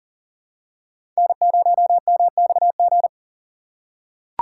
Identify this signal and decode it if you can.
Morse code, heard as a keyed tone, about 39 wpm, 700 Hz. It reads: N0MXG
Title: USA Call Signs - 5 Characters - 4x Character Spacing - 40wpm